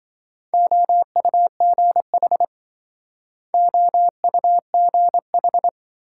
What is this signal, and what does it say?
Morse code, heard as a keyed tone, about 26 wpm, 700 Hz. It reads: OUGH OUGH